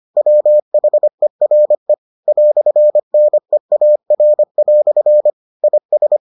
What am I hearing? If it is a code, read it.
Morse code, heard as a keyed tone, about 25 wpm, 600 Hz. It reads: WHERE "NEAR" IS